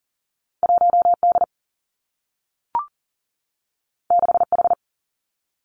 Morse code, heard as a keyed tone, about 40 wpm, 700 Hz. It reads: 1 D 6 H
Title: Two Challenging Letter-Number Combinations 40wpm